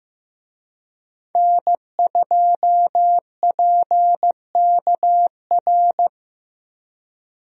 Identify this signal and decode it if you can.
Morse code, heard as a keyed tone, about 15 wpm, 700 Hz. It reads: N2PKR
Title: USA Call Signs - 5 Characters - Repeated 3x - 15wpm